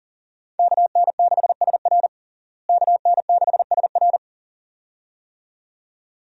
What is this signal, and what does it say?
Morse code, heard as a keyed tone, about 37 wpm, 700 Hz. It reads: KN6SR KN6SR
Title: USA Call Signs - 5 Characters - Repeated 2x - 40wpm